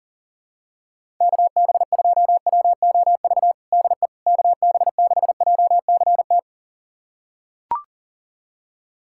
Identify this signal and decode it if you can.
Morse code, heard as a keyed tone, about 40 wpm, 700 Hz. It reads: KB2WOV DE KB6JCT
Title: USA Callsign DE Callsign - 6 Characters - 40wpm